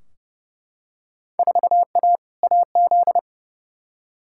Morse code, heard as a keyed tone, about 32 wpm, 700 Hz. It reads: 4A AZ
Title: Field Day Exchanges - 30wpm